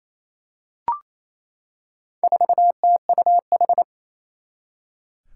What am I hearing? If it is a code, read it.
Morse code, heard as a keyed tone, about 27 wpm, 700 Hz. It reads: 4TUH
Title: Four Challenging Letter-Number Combinations 28wpm